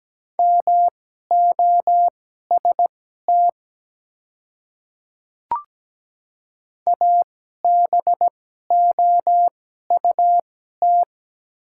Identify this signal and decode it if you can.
Morse code, heard as a keyed tone, about 17 wpm, spaced as for 12 wpm, 700 Hz. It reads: MOST ABOUT
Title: 2x Character Spacing - Top 100 Words - 17wpm